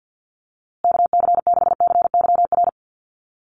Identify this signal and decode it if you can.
Morse code, heard as a keyed tone, about 50 wpm, 700 Hz. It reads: KC6CYR
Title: USA Call Signs Repeated 2x 50wpm